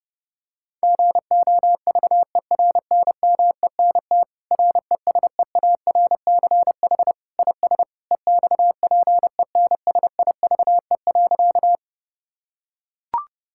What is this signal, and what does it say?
Morse code, heard as a keyed tone, about 30 wpm, 700 Hz. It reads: GOVERNMENT RESEARCH IS EXPENSIVE.
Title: Sentences from Top 1000 Words - Review Part 6 - 30wpm